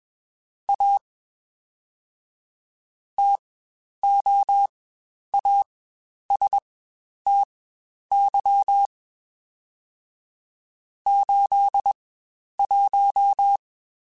Morse code, heard as a keyed tone, about 21 wpm, spaced as for 10 wpm, 800 Hz. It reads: A TOASTY 81